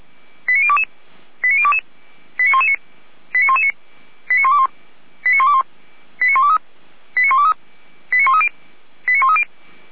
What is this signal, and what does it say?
Five-tone selective call (ZVEI), heard as quick runs of five tones, tones of about 70 ms. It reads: 89022 89022 89119 89119 89121 89121 89123 89123 89130 89130
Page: ZVEI